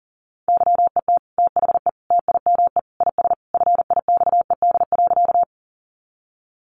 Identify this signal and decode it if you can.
Morse code, heard as a keyed tone, about 40 wpm, 700 Hz. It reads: YET THE TIME IS FIXED.
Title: Sentences from Top 1100 Words 40wpm